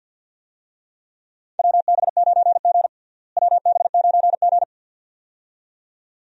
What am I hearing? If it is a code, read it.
Morse code, heard as a keyed tone, about 52 wpm, 700 Hz. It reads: WB9G WB9G
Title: USA Call Signs - 4 Characters - Repeated 2x - 50wpm